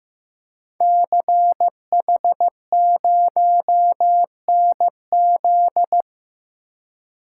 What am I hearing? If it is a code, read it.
Morse code, heard as a keyed tone, about 15 wpm, 700 Hz. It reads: CH0NZ
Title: Code Groups - 5 Letter:Numbers - 15wpm